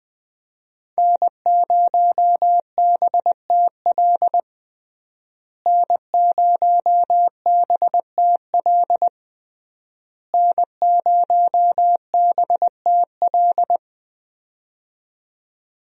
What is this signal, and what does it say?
Morse code, heard as a keyed tone, about 20 wpm, 700 Hz. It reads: N0BTL N0BTL N0BTL
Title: USA Call Signs - 5 Characters - Repeated 3x - 20wpm